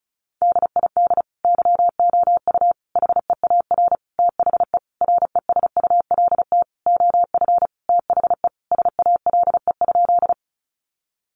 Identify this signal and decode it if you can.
Morse code, heard as a keyed tone, about 34 wpm, 700 Hz. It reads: DID YOU HEAR THE RESULT OF THE SALE?
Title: Sentences from Top 1400 Words 35wpm